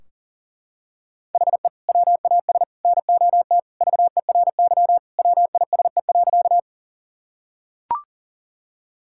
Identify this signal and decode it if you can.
Morse code, heard as a keyed tone, about 37 wpm, 700 Hz. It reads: HE WAS NOT VERY WISE.
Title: Sentences from Top 1600 Words 40wpm